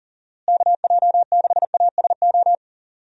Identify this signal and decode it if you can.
Morse code, heard as a keyed tone, about 40 wpm, 700 Hz. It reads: KJ6ASO
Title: USA Call Signs - 6 Characters - Repeated 3x - 40wpm